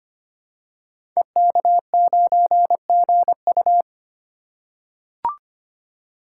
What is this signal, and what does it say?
Morse code, heard as a keyed tone, about 25 wpm, 700 Hz. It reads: EK9GU
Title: Code Groups - 5 Letter:Numbers - 25wpm